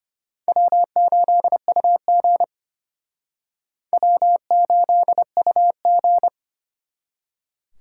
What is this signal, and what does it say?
Morse code, heard as a keyed tone, about 31 wpm, 700 Hz. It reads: W8UG W8UG
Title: Speed Racing - USA Call Signs 25wpm